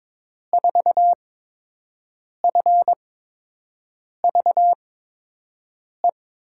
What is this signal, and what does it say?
Morse code, heard as a keyed tone, about 22 wpm, spaced as for 6 wpm, 700 Hz. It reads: 4FVE
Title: USA Call Signs - 6 Characters - 8x Character Spacing - 22wpm